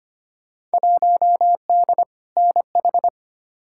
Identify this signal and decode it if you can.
Morse code, heard as a keyed tone, about 25 wpm, 700 Hz. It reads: 1D NH
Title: Field Day Exchanges - 25wpm